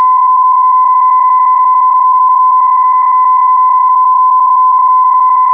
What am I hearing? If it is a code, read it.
PSK31, about 31.25 baud.